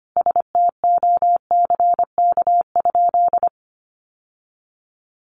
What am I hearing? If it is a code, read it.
Morse code, heard as a keyed tone, about 25 wpm, 700 Hz. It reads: STOCK?
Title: Sentences from Top 800 Words 25wpm